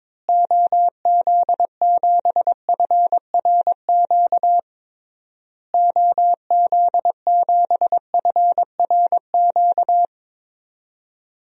Morse code, heard as a keyed tone, about 22 wpm, 700 Hz. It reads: OZ7FRQ OZ7FRQ
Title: DXCC-100 Call Signs - 6 Characters - Repeated 3x - 22wpm